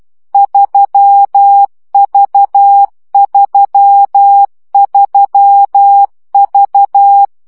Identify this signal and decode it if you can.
Morse code, heard as a keyed tone, about 12 wpm, 800 Hz. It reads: 3V33V
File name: CW gyakorlas v 3 10 baud